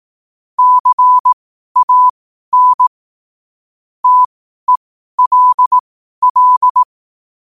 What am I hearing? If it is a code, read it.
Morse code, heard as a keyed tone, about 18 wpm, spaced as for 12 wpm, 1000 Hz. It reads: CAN TELL
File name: qotdfiles / 12wpm